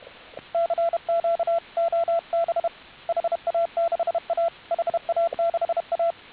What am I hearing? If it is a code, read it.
Morse code, heard as a keyed tone, about 31 wpm, 690 Hz, with radio noise. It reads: CQ OB HA6A HA6A